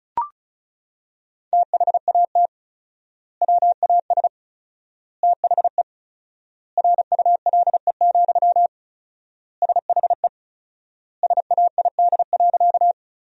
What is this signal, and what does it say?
Morse code, heard as a keyed tone, about 36 wpm, 700 Hz. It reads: THAT WAS THE RULE, SHE SAID.
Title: 4x Word-Spacing - Sentences from Top 400 Words - 35wpm